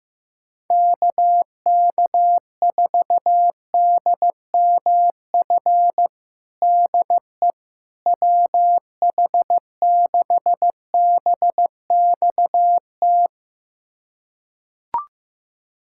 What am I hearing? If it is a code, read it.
Morse code, heard as a keyed tone, about 15 wpm, 700 Hz. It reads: KK4DMF DE WH6BXT
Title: USA Callsign DE Callsign 15wpm